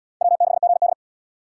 Morse code, heard as a keyed tone, about 70 wpm, 700 Hz. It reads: W6GD